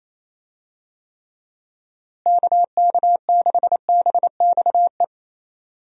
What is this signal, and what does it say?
Morse code, heard as a keyed tone, about 28 wpm, 700 Hz. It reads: KK6BXE
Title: USA Call Signs - 6 Characters - Repeated 3x - 28wpm